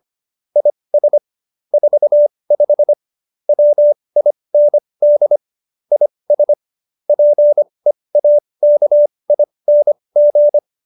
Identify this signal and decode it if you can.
Morse code, heard as a keyed tone, about 25 wpm, spaced as for 20 wpm, 590 Hz. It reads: IS 45 WIND IS PEAKING